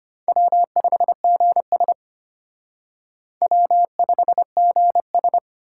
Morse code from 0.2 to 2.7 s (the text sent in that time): W5GS